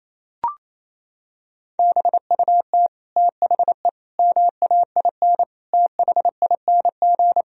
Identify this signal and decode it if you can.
Morse code, heard as a keyed tone, about 27 wpm, 700 Hz. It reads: BUT THE MAIN THING
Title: Sentences from Top 1000 Words - Review Part 2 - 28wpm